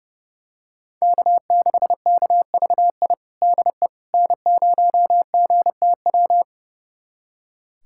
Morse code, heard as a keyed tone, about 30 wpm, 700 Hz. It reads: K6KVI DE N0GTW
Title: USA Callsign DE Callsign - 5 Characters - 30wpm